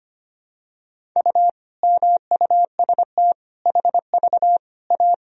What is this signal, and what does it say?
Morse code, heard as a keyed tone, about 25 wpm, 700 Hz. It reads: U MUST HV A
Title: Sentences from Top 500 Words and Abbreviations 25wpm